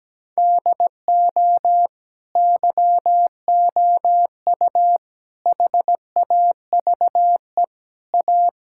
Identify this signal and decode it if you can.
Morse code, heard as a keyed tone, about 17 wpm, 700 Hz. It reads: DO YOU HAVE A
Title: Sentences from Top 1000 Words - Review Part 6 - 17wpm